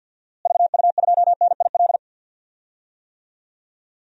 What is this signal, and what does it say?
Morse code, heard as a keyed tone, about 53 wpm, 700 Hz. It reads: VU2NIL
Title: DXCC-100 Call Signs - 6 Characters - Repeated 2x - 50wpm